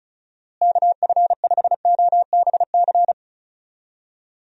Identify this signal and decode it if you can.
Morse code, heard as a keyed tone, about 34 wpm, 700 Hz. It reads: KF5OBC
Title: USA Call Signs - 6 Characters - Repeated 2x - 35wpm